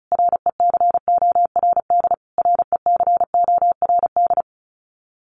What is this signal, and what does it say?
Morse code, heard as a keyed tone, about 34 wpm, 700 Hz. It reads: RECORD RECORD
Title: Top 500 Words Repeated 35wpm